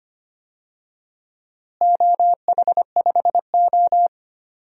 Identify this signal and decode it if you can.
Morse code, heard as a keyed tone, about 25 wpm, 700 Hz. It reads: OH5O
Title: DXCC-100 Call Signs - 4 Characters - Repeated 2x - 25wpm